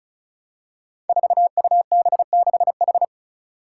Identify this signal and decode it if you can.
Morse code, heard as a keyed tone, about 36 wpm, 700 Hz. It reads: 4UB6H
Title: Five Challenging Letter-Number Combinations 35wpm